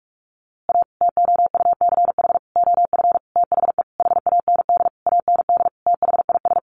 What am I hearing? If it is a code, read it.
Morse code, heard as a keyed tone, about 44 wpm, 700 Hz. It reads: A TOUCH OF THE HAND AND THIS
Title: A Sound Of Thunder 45wpm